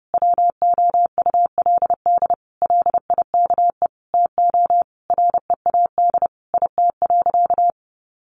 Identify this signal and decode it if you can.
Morse code, heard as a keyed tone, about 30 wpm, 700 Hz. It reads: WOULD LIKE TO READ IT.
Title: Sentences from Top 500 Words 30wpm